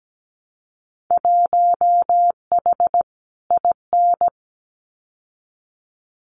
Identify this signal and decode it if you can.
Morse code, heard as a keyed tone, about 17 wpm, 700 Hz. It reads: 1H IN
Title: Winter Field Day Exchanges 17wpm